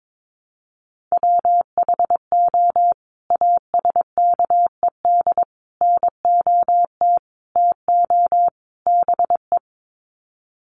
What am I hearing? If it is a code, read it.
Morse code, heard as a keyed tone, about 22 wpm, 700 Hz. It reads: WHO ASKED NOT TO BE
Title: Sets of 5 Words Encore2 22wpm